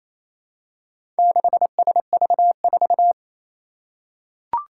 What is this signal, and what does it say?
Morse code, heard as a keyed tone, about 27 wpm, 700 Hz. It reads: 6SV4
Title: Four Challenging Letter-Number Combinations 28wpm